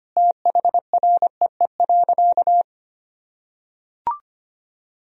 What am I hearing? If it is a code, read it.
Morse code, heard as a keyed tone, about 25 wpm, 700 Hz. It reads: THREE.
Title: Sentences from Top 1000 Words - Review Part 2 - 25wpm